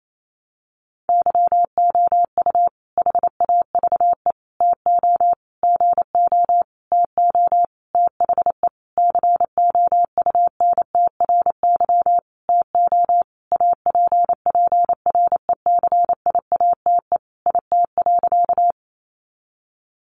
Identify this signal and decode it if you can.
Morse code, heard as a keyed tone, about 28 wpm, 700 Hz. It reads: YOU HAVE TO GO TO THE COUNTRY TO APPRECIATE IT.
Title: Sentences from Top 1300 Words 28wpm